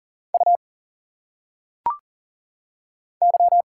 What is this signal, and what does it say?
Morse code, heard as a keyed tone, about 40 wpm, 700 Hz. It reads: U Y
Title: Single Letters - Phonetic Pronunciation 40wpm